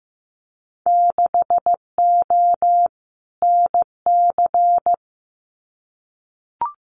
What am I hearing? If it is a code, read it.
Morse code, heard as a keyed tone, about 15 wpm, 700 Hz. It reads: 6O NC
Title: Winter Field Day Exchanges 15wpm